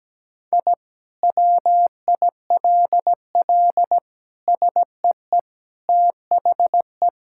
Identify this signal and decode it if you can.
Morse code, heard as a keyed tone, about 17 wpm, 700 Hz. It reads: I WILL SEE THE
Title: Sentences from Top 1000 Words - Review Part 6 - 17wpm